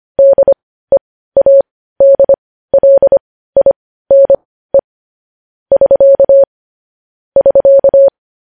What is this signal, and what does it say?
Morse code, heard as a keyed tone, about 25 wpm, spaced as for 15 wpm, 550 Hz. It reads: DEADLINE <SK> <SK>
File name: podcast / News headlines-1x-25x15